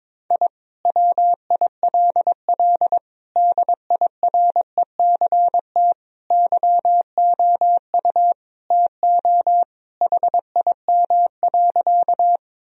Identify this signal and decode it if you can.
Morse code, heard as a keyed tone, about 22 wpm, 700 Hz. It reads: I WILL DIRECT YOU TO HIM.
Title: Sentences from Top 1000 Words - Review Part 2 - 22wpm